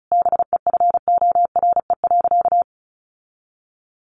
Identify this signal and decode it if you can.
Morse code, heard as a keyed tone, about 34 wpm, 700 Hz. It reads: BEFORE.
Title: Sentences from Top 1100 Words 35wpm